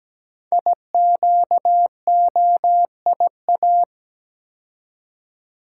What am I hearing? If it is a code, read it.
Morse code, heard as a keyed tone, about 17 wpm, 700 Hz. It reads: IQOIA
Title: Code Groups - 5 Letters - 17wpm